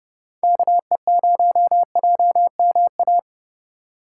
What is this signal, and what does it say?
Morse code, heard as a keyed tone, about 32 wpm, 700 Hz. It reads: KE0JMA
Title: USA Call Signs - 6 Characters - 3x Character Spacing - 30wpm